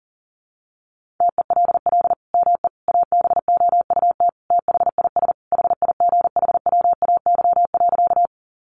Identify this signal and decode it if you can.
Morse code, heard as a keyed tone, about 40 wpm, 700 Hz. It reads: TELL ME ABOUT THIS HIGHWAY.
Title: Sentences from Top 1400 Words 40wpm